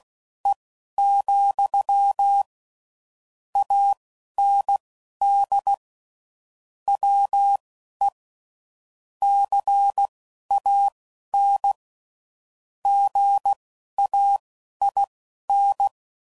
Morse code, heard as a keyed tone, about 16 wpm, spaced as for 11 wpm, 800 Hz. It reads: E, AND WE CAN GAIN
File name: CW Code Groups @10WPM / Technical Subjects / SwPwrSup